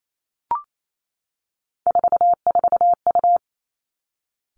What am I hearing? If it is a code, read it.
Morse code, heard as a keyed tone, about 28 wpm, 700 Hz. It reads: 44U
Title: Three Challenging Letter-Number Combinations 28wpm